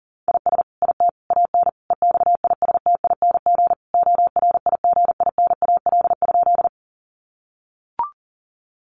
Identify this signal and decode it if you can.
Morse code, heard as a keyed tone, about 37 wpm, 700 Hz. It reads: IS IT AN EXISTING ORIGINAL?
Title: Sentences from Top 1500 Words 40wpm